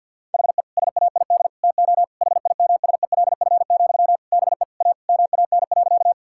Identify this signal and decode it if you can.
Morse code, heard as a keyed tone, about 50 wpm, 700 Hz. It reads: HE SAID TO HIMSELF, BE A MAN.
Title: Sentences from Top 1600 Words 50wpm